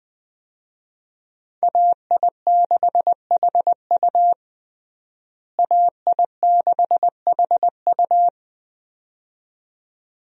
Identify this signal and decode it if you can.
Morse code, heard as a keyed tone, about 20 wpm, 700 Hz. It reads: AI6HU AI6HU
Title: USA Call Signs - 5 Characters - Repeated 2x - 20wpm